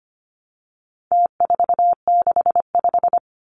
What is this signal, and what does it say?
Morse code, heard as a keyed tone, about 25 wpm, 700 Hz. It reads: T465
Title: Numbers - Contest Style - 4-Digit 25wpm